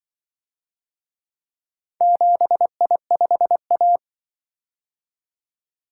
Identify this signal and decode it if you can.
Morse code, heard as a keyed tone, about 25 wpm, 700 Hz. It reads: 7I5A